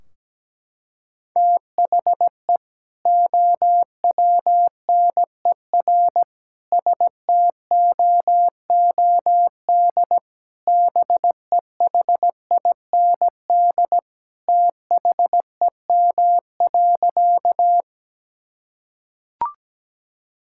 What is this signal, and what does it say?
Morse code, heard as a keyed tone, about 17 wpm, 700 Hz. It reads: THE OWNER STOOD BEHIND THEM.
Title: Sentences from Top 1600 Words 17wpm